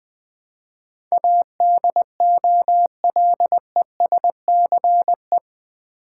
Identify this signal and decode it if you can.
Morse code, heard as a keyed tone, about 21 wpm, 700 Hz. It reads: ADOLESCE
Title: Build A Word - 4 Letters At A Time - 10-Letter Words - 20wpm